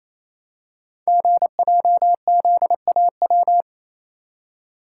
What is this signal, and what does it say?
Morse code, heard as a keyed tone, about 27 wpm, 700 Hz. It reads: GJZAW
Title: Code Groups - 5 Letters - 28wpm